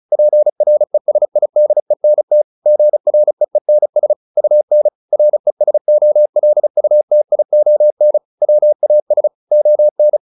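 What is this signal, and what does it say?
Morse code, heard as a keyed tone, about 34 wpm, 600 Hz. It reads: PRESIDENT GREENS UN RESOLUTION WAS ON
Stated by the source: Morse practice at 35 wpm from CW med Gnister, sending Ken Follett's novel Never in English.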